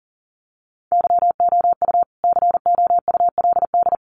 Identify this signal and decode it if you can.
Morse code, heard as a keyed tone, about 40 wpm, 700 Hz. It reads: YOU COULD
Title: Sets of 2 Words 40wpm